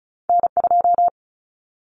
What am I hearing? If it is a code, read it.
Morse code, heard as a keyed tone, about 36 wpm, 700 Hz. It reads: N2
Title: Common Cut Numbers - 2-Digit 35wpm